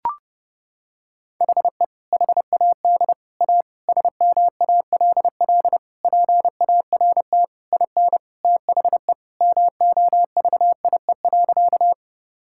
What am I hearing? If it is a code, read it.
Morse code, heard as a keyed tone, about 30 wpm, 700 Hz. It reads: HE HAD A SMALL PART IN THE MOVIE.
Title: Sentences from Top 1000 Words - Review Part 5 - 30wpm